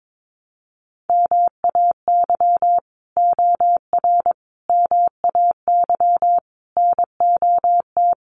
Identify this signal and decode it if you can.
Morse code, heard as a keyed tone, about 22 wpm, 700 Hz. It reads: MAY OR MAY NOT
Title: Sets of 5 Words Encore2 22wpm